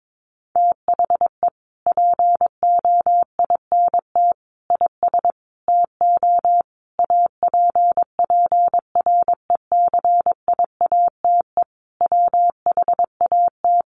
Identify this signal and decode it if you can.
Morse code, heard as a keyed tone, about 22 wpm, 700 Hz. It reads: THE POINT IS TO APPRECIATE WHAT
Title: Sentences from Top 1300 Words 22wpm